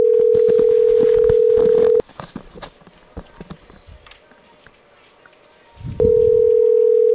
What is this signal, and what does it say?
Ringback tone (North America), 440 + 480 Hz.